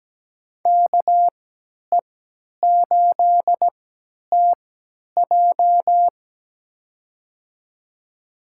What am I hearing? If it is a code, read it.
Morse code, heard as a keyed tone, about 17 wpm, spaced as for 10 wpm, 700 Hz. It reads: KE8TJ
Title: USA Call Signs - 5 Characters - 3x Character Spacing - 17wpm